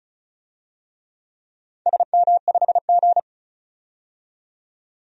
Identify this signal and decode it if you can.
Morse code, heard as a keyed tone, about 34 wpm, 700 Hz. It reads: SM5G
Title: DXCC-100 Call Signs - 4 Characters - 35wpm